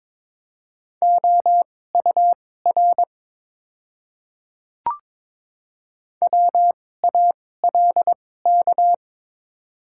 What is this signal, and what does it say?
Morse code, heard as a keyed tone, about 22 wpm, spaced as for 16 wpm, 700 Hz. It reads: OUR WALK
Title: 2x Character Spacing - Top 300 Words - 22wpm